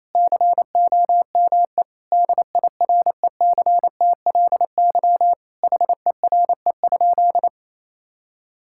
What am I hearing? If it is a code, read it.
Morse code, heard as a keyed tone, about 27 wpm, 700 Hz. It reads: COME DIRECTLY HERE?
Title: Sentences from Top 1000 Words - Review Part 5 - 28wpm